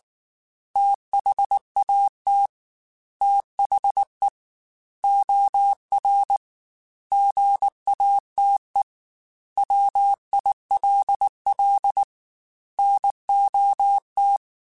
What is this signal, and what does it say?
Morse code, heard as a keyed tone, about 19 wpm, 800 Hz. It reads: THAT THE OR GATE WILL NOT
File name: Technical Subjects / Technical Subjects-mp3s / OR Gate @18